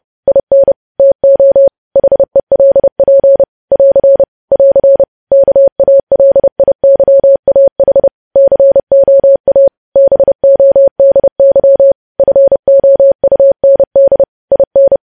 Morse code, heard as a keyed tone, about 30 wpm, 550 Hz. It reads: IN TO HELP <AR> <AR> KALIYAH COA BODY FOUND IN
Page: news headlines in morse code at 30 wpm